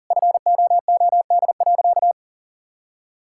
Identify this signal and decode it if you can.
Morse code, heard as a keyed tone, about 40 wpm, 700 Hz. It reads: FOOD.
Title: Sentences from Top 1000 Words - Review Part 5 - 40wpm